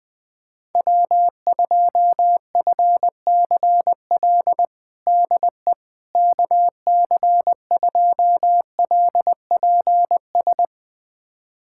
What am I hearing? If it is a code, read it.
Morse code, heard as a keyed tone, about 20 wpm, 700 Hz. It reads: W2FCL DE KC2LPS
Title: USA Callsign DE Callsign 20wpm